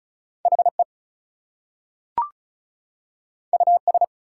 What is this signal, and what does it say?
Morse code, heard as a keyed tone, about 36 wpm, 700 Hz. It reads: HE US
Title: Two Letter Words - 35wpm